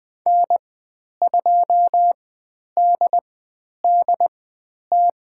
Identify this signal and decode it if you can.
Morse code, heard as a keyed tone, about 20 wpm, spaced as for 10 wpm, 700 Hz. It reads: N2DDT
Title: 095 Lesson 16g - Letter c - Call Signs 20wpm